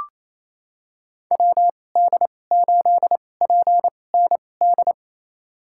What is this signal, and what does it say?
Morse code, heard as a keyed tone, about 28 wpm, spaced as for 20 wpm, 700 Hz. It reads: WD8PND
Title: USA Call Signs - 6 Characters - 2x Character Spacing - 28wpm